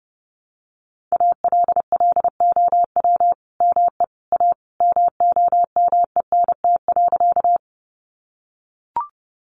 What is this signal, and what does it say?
Morse code, heard as a keyed tone, about 30 wpm, 700 Hz. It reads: ALLOW ME A MOMENT.
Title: Sentences from Top 1000 Words 30wpm